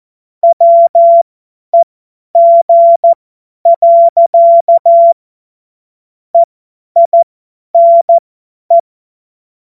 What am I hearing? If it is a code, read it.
Morse code, heard as a keyed tone, about 14 wpm, spaced as for 10 wpm, 680 Hz. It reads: WEG. EINE
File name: Morsekurs / Witze / Witz 11. Januar 2022, 50 F70